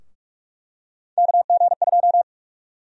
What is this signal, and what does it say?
Morse code, heard as a keyed tone, about 41 wpm, 700 Hz. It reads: KG2
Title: USA Call Sign Prefixes - Repeated 2x - 45wpm